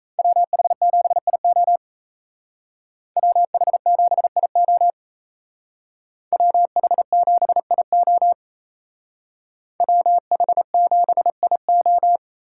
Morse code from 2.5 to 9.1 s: WH7IO WH7IO

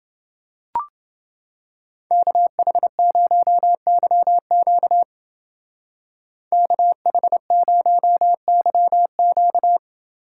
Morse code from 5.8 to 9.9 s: KH0YQ

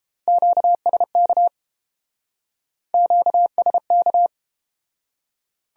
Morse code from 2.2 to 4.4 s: QSK

